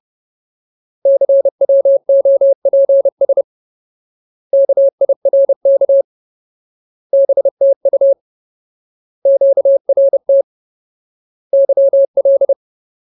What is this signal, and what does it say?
Morse code, heard as a keyed tone, about 30 wpm, 550 Hz. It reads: CWOPS KIRK BTU QRT YL
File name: Ultimate Word List / 30WPM Ultimate